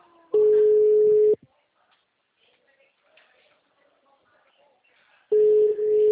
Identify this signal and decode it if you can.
Ringback tone (Europe, 425 Hz), with speech.
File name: out- -2021- - - .43512